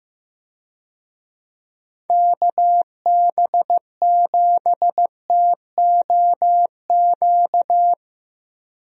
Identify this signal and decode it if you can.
Morse code, heard as a keyed tone, about 15 wpm, 700 Hz. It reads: KB7TOQ
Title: USA Call Signs - 6 Characters - Repeated 2x - 15wpm